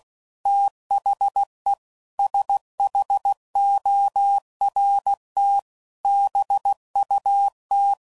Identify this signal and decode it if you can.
Morse code, heard as a keyed tone, about 16 wpm, 800 Hz. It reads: THE SHORT BUT
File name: CW Code Groups @15WPM / QSOs / WA7QOX QRX